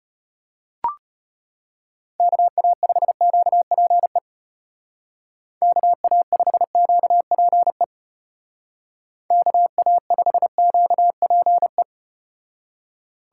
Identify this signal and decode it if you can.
Morse code, heard as a keyed tone, about 33 wpm, 700 Hz. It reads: KA5QPE KA5QPE KA5QPE